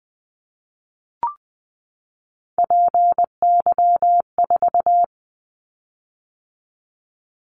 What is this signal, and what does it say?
Morse code, heard as a keyed tone, about 20 wpm, 700 Hz. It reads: PY4
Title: Callsign Prefixes for Top 100 DXCC - 20wpm